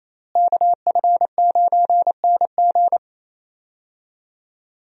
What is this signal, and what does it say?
Morse code, heard as a keyed tone, about 28 wpm, 700 Hz. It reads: KF9NG
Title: USA Call Signs - 5 Characters - Repeated 3x - 28wpm